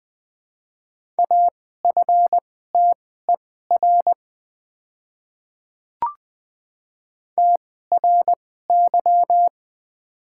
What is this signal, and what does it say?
Morse code, heard as a keyed tone, about 20 wpm, spaced as for 14 wpm, 700 Hz. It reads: AFTER TRY
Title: 2x Character Spacing - Top 200 Words - 20wpm